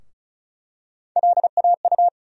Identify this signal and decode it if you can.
Morse code, heard as a keyed tone, about 36 wpm, 700 Hz. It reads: LAU
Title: USA Call Sign Postfixes - 3 Letters 35wpm